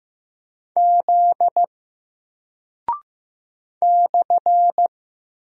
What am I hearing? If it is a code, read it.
Morse code, heard as a keyed tone, about 15 wpm, 700 Hz. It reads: Z /
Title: Single Character (Letter:Number:Punctuation:Prosign) Alt 15wpm